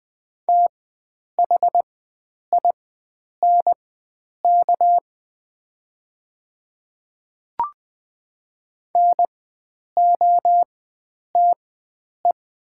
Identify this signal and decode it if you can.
Morse code, heard as a keyed tone, about 20 wpm, spaced as for 9 wpm, 700 Hz. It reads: THINK NOTE
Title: 4x Character Spacing - Top 500 Words - 20wpm